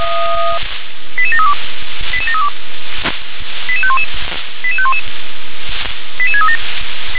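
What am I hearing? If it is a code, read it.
Five-tone selective call (ZVEI), heard as quick runs of five tones, tones of about 70 ms. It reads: 88732 88732 88511 88511 88637
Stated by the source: SDR-received ZVEI five-tone call 88500.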